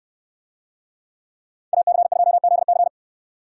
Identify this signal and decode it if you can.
Morse code, heard as a keyed tone, about 67 wpm, 700 Hz. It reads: AC2ZC